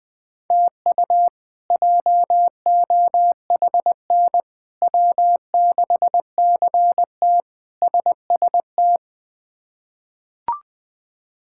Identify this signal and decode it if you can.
Morse code, heard as a keyed tone, about 21 wpm, 700 Hz. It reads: TU JOHN W6CT SST
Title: K1USN SST QSO Practice - 20wpm